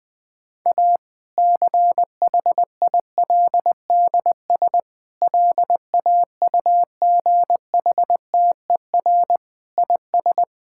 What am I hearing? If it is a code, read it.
Morse code, heard as a keyed tone, about 20 wpm, 700 Hz. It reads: A CHILDS LAUGHTER IS
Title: Difficult Sentences with Hints - 20wpm